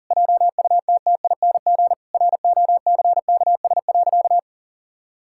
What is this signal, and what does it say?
Morse code, heard as a keyed tone, about 40 wpm, 700 Hz. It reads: JUTTING ROCKS.
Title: Pangram Challenge v2 - 40wpm